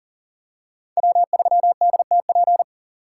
Morse code, heard as a keyed tone, about 40 wpm, 700 Hz. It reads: W3DTP
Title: USA Call Signs - 5 Characters - 4x Character Spacing - 40wpm